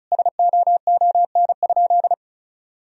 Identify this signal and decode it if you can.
Morse code, heard as a keyed tone, about 36 wpm, 700 Hz. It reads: SOON?
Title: Sentences from Top 400 Words and Abbreviations 35wpm